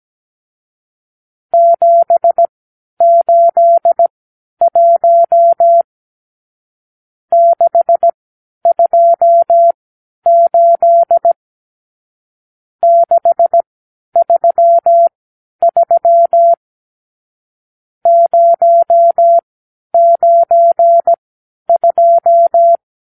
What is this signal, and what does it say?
Morse code, heard as a keyed tone, about 17 wpm, spaced as for 10 wpm, 680 Hz. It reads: 781 628 633 092
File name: arrl / Review (1-0)